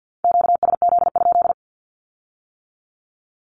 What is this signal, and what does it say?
Morse code, heard as a keyed tone, about 50 wpm, 700 Hz. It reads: QSZ?
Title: Q-Codes with Spoken Meaning - 50wpm